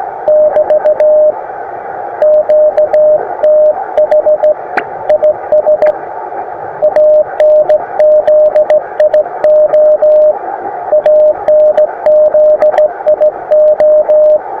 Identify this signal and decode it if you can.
Morse code, heard as a keyed tone, about 15 wpm, 610 Hz. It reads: <BT> QTH IS ANZIO ANZIO
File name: attachments / iz0aym 20m